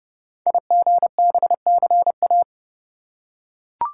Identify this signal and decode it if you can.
Morse code, heard as a keyed tone, about 30 wpm, 700 Hz. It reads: IGBCA
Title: Code Groups - 5 Letters - 30wpm